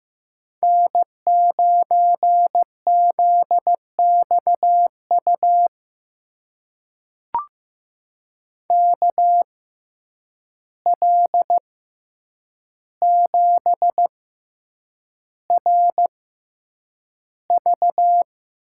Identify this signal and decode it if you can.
Morse code, heard as a keyed tone, about 16 wpm, spaced as for 8 wpm, 700 Hz. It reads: N9ZXU KL7RV
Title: USA Call Signs - 5 Characters - 6x Character Spacing - 15wpm